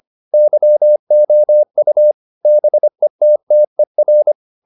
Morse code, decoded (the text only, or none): YOU BETTER